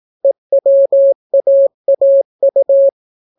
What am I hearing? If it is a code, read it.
Morse code, heard as a keyed tone, about 18 wpm, 550 Hz. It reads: EWAAU